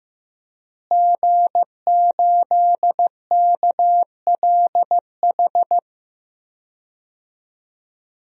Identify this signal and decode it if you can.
Morse code, heard as a keyed tone, about 15 wpm, 700 Hz. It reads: G8KLH
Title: DXCC-100 Call Signs - 5 Characters - Repeated 3x - 15wpm